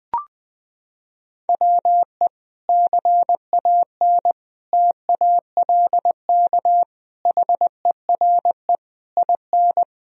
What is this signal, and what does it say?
Morse code, heard as a keyed tone, about 21 wpm, 700 Hz. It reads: WE CAN TALK HERE IN
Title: Sentences from Top 1000 Words - Review Part 4 - 20wpm